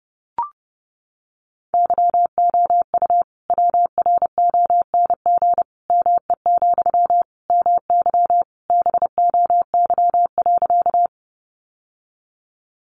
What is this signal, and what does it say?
Morse code, heard as a keyed tone, about 30 wpm, 700 Hz. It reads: YOU WRONG ME, MY BOY.
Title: Sentences from Top 900 Words 30wpm